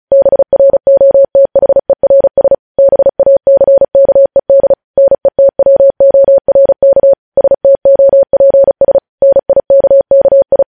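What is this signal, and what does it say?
Morse code, heard as a keyed tone, about 36 wpm, 550 Hz. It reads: BROTHERS BACKED NETWORK STOPS NIKKI
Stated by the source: Morse practice sending news headlines at 35 wpm.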